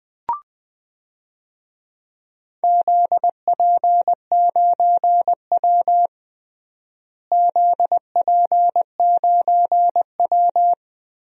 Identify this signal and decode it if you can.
Morse code, heard as a keyed tone, about 20 wpm, 700 Hz. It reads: ZP9W ZP9W
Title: DXCC-100 Call Signs - 4 Characters - Repeated 3x - 20wpm